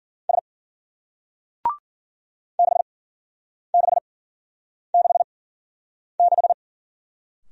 Morse code, heard as a keyed tone, about 56 wpm, spaced as for 9 wpm, 700 Hz. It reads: S 6666